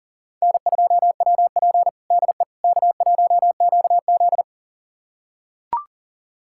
Morse code, heard as a keyed tone, about 40 wpm, 700 Hz. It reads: N2WP DE K1QZ
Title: USA Callsign DE Callsign - 4 Characters - 40wpm